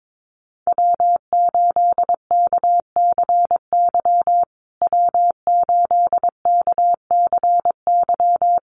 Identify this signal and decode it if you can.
Morse code, heard as a keyed tone, about 22 wpm, 700 Hz. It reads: W8KCY W8KCY
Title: USA Call Signs Repeated 3x 22wpm